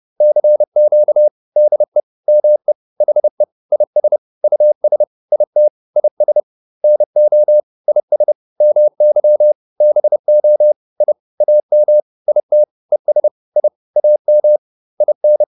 Morse code, heard as a keyed tone, about 30 wpm, 600 Hz. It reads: CQ DE ME HE IS US IT IS NO IS MY BO I AM IT ES I AM IN